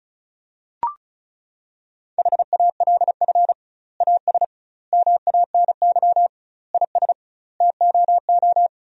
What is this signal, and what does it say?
Morse code, heard as a keyed tone, about 36 wpm, 700 Hz. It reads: HALF AS MANY IS TOO
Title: 2x Word-Spacing - Sentences from Top 400 Words - 35wpm